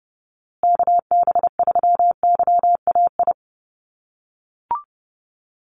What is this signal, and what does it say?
Morse code, heard as a keyed tone, about 30 wpm, 700 Hz. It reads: KB3YAI
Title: USA Call Signs Repeated 2x 30wpm